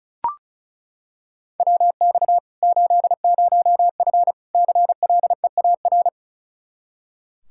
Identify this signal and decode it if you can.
Morse code, heard as a keyed tone, about 36 wpm, 700 Hz. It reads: WX 80F CLEAR
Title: Practice QSO - Exchange 2 - Info - 35wpm